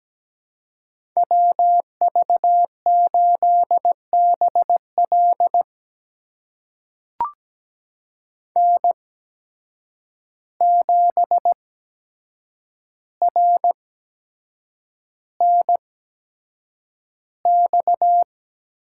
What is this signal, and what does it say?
Morse code, heard as a keyed tone, about 17 wpm, spaced as for 7 wpm, 700 Hz. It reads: WV8BL N7RNX
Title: USA Call Signs - 5 Characters - 8x Character Spacing - 17wpm